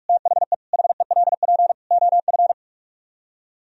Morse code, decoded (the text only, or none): THE HELP OF